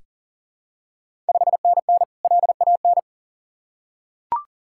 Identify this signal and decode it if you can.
Morse code, heard as a keyed tone, about 39 wpm, 690 Hz. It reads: 5NN LAN
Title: Ontario QSO Party Exchange 40wpm